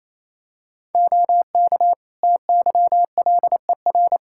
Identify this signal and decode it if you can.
Morse code, heard as a keyed tone, about 28 wpm, 700 Hz. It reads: OK TYLER
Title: Practice QSO - Exchange 3 - More Info - 28wpm